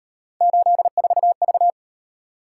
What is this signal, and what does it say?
Morse code, heard as a keyed tone, about 37 wpm, 700 Hz. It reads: 84V